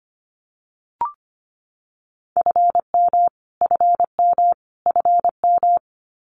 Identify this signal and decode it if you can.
Morse code, heard as a keyed tone, about 25 wpm, 700 Hz. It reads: FM FM FM
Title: Abbreviations Repeated 3x - 25wpm